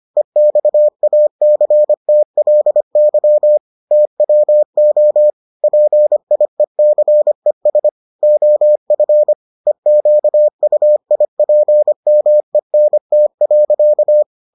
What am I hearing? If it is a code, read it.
Morse code, heard as a keyed tone, about 25 wpm, 600 Hz. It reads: EXACTLY TWO PIECES OF EQUIPMENT.